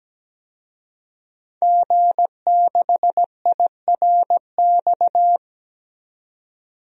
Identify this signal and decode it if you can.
Morse code, heard as a keyed tone, about 17 wpm, 700 Hz. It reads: G6IRX
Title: DXCC-100 Call Signs - 5 Characters - 17wpm